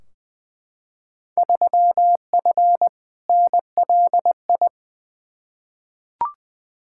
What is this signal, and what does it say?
Morse code, heard as a keyed tone, about 20 wpm, 700 Hz. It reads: 3F NLI
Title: Field Day Exchanges Encore - 20wpm